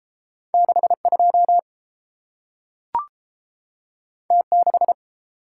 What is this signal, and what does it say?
Morse code, heard as a keyed tone, about 30 wpm, 700 Hz. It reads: 62 T6